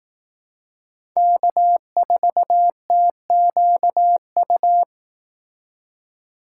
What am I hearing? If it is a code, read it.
Morse code, heard as a keyed tone, about 18 wpm, 700 Hz. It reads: K4TQU
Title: Speed Racing - USA 1x3 Call Signs 15wpm